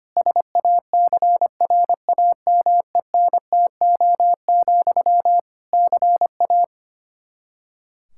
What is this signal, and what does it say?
Morse code, heard as a keyed tone, about 25 wpm, 700 Hz. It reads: SACRAMENTO, CA
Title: Practice QSO - Exchange 1 - Beginning - 25wpm